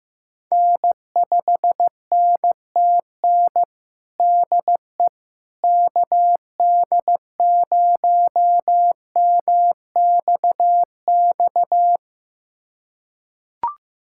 Morse code, heard as a keyed tone, about 15 wpm, 700 Hz. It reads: N5NTN DE KD0MXX
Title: USA Callsign DE Callsign 15wpm